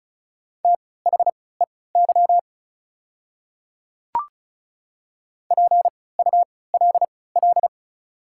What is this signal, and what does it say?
Morse code, heard as a keyed tone, about 34 wpm, spaced as for 20 wpm, 690 Hz. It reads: THEY PULL
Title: 3x Character Spacing - Top 500 Words - 35wpm